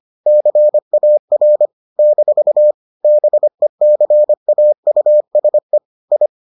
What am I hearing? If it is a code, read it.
Morse code, heard as a keyed tone, about 25 wpm, 600 Hz. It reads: CAR - BECAUSE I